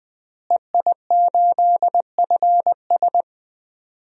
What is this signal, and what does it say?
Morse code, heard as a keyed tone, about 21 wpm, 700 Hz. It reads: EI8FS
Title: DXCC-100 Call Signs - 5 Characters - Repeated 3x - 20wpm